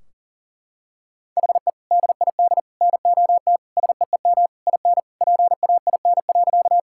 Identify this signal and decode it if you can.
Morse code, heard as a keyed tone, about 40 wpm, 700 Hz. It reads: HE DID NOT SEEM IN PAIN.
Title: Sentences from Top 1600 Words 40wpm